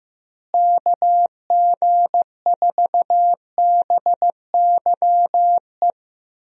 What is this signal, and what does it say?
Morse code, heard as a keyed tone, about 15 wpm, 700 Hz. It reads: KG4BYE
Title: USA Call Signs - 6 Characters - Repeated 3x - 15wpm